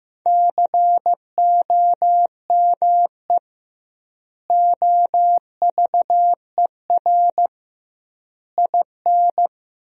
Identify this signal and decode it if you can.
Morse code, heard as a keyed tone, about 16 wpm, 700 Hz. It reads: COME OVER IN
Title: 2x Word-Spacing - Sentences from Top 300 Words - 15wpm